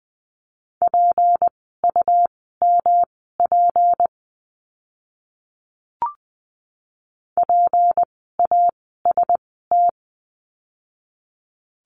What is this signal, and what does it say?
Morse code, heard as a keyed tone, about 20 wpm, spaced as for 14 wpm, 700 Hz. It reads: PUMP PAST
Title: 153 Lesson 25f - Letter p - Words 20wpm